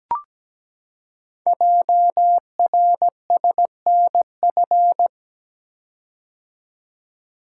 Morse code, heard as a keyed tone, about 17 wpm, 700 Hz. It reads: JRSNF
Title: Code Groups - 5 Letters - 17wpm